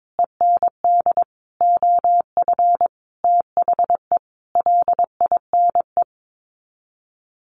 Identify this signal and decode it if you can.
Morse code, heard as a keyed tone, about 22 wpm, 700 Hz. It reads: END OF THE LINE
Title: Sets of 5 Words Encore2 22wpm